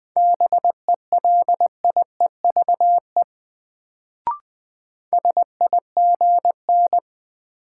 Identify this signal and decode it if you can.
Morse code, heard as a keyed tone, about 20 wpm, 700 Hz. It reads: BELIEVE SIGN
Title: Sentences from Top 1000 Words - New Word Drill - 20wpm